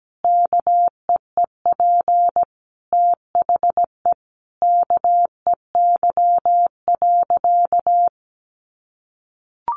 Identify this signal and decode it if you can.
Morse code, heard as a keyed tone, about 17 wpm, 700 Hz. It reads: KEEP THE KEY.
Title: Sentences from Top 1300 Words 17wpm